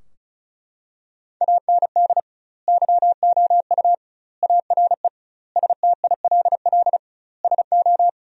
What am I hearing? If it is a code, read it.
Morse code, heard as a keyed tone, about 36 wpm, 700 Hz. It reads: AND YOU ARE STILL SO
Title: 2x Word-Spacing - Sentences from Top 400 Words - 35wpm